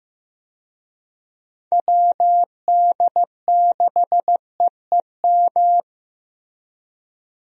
Morse code, heard as a keyed tone, about 16 wpm, 700 Hz. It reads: WD6EEM
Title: USA Call Signs - 6 Characters - Repeated 2x - 15wpm